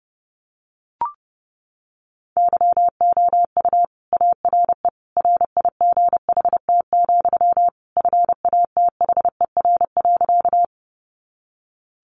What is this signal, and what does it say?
Morse code, heard as a keyed tone, about 30 wpm, 700 Hz. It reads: YOU ARE RIGHT, FATHER.
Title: Sentences from Top 300 Words 30wpm